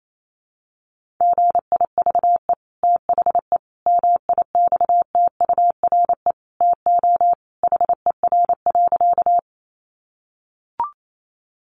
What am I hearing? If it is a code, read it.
Morse code, heard as a keyed tone, about 28 wpm, 700 Hz. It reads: GIVE THE MIXTURE TO HER.
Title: Sentences from Top 1500 Words 28wpm